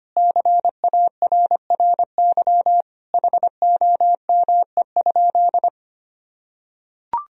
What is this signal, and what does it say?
Morse code, heard as a keyed tone, about 25 wpm, 700 Hz. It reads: CARRY HOME?
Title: Sentences from Top 1000 Words - Review Part 1 - 25wpm